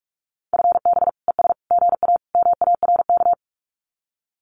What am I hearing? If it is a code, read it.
Morse code, heard as a keyed tone, about 46 wpm, 700 Hz. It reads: FB ES GA MARK
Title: Common QSO Phrases Encore 45wpm